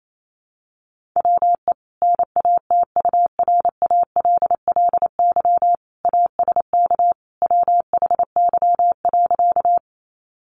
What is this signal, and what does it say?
Morse code, heard as a keyed tone, about 28 wpm, 700 Hz. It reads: WE NATURALLY ASK WHY.
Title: Sentences from Top 1300 Words 28wpm